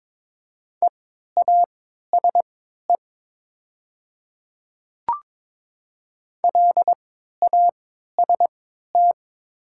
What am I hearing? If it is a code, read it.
Morse code, heard as a keyed tone, about 22 wpm, spaced as for 12 wpm, 700 Hz. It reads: EASE LAST
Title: 3x Character Spacing - Top 300 Words - 22wpm